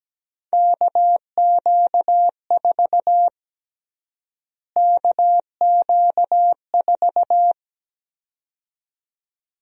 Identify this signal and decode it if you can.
Morse code, heard as a keyed tone, about 17 wpm, 700 Hz. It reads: KQ4 KQ4
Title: USA Call Sign Prefixes - Repeated 2x - 17wpm